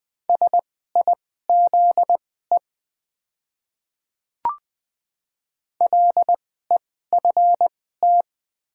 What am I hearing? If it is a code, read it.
Morse code, heard as a keyed tone, about 20 wpm, spaced as for 14 wpm, 700 Hz. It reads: SIZE LEFT
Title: 2x Character Spacing - Top 500 Words - 20wpm